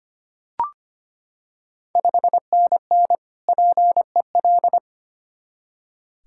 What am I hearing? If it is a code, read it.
Morse code, heard as a keyed tone, about 25 wpm, 700 Hz. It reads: 5NN PEL
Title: Ontario QSO Party Exchange 25wpm